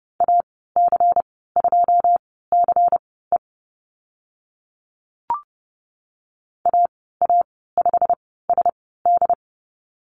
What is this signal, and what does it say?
Morse code, heard as a keyed tone, about 30 wpm, spaced as for 17 wpm, 700 Hz. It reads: AC2CE AA5SD
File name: Lesson 19g - Letter w - Call Signs 30wpm